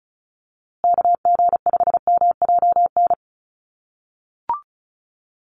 Morse code, heard as a keyed tone, about 34 wpm, 700 Hz. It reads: KG5MJN
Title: USA Call Signs Repeated 2x 35wpm